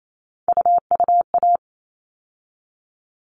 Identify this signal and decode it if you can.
Morse code, heard as a keyed tone, about 28 wpm, 700 Hz. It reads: UUA
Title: All Cut Numbers - 3-Digit 28wpm